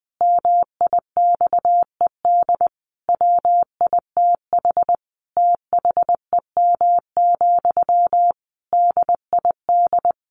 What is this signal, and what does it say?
Morse code, heard as a keyed tone, about 19 wpm, 700 Hz. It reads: MIXED WITH THEM, DID
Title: Sentences from Top 1500 Words 20wpm